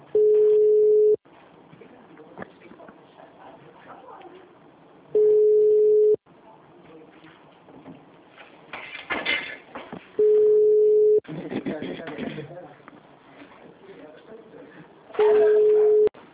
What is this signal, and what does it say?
Ringback tone (Europe, 425 Hz), with speech.
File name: out- -2006- - - .31745